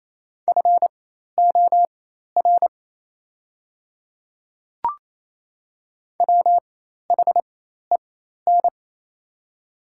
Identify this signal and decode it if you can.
Morse code, heard as a keyed tone, about 28 wpm, spaced as for 13 wpm, 690 Hz. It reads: FOR WHEN
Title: 4x Character Spacing - Top 300 Words - 28wpm